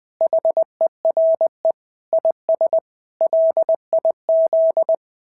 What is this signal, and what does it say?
Morse code, heard as a keyed tone, about 20 wpm, 690 Hz. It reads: HERE IS LIZ
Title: SOTA QSO Practice 20wpm